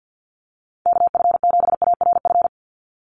Morse code, heard as a keyed tone, about 50 wpm, 700 Hz. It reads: KF7ARF
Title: USA Call Signs 50wpm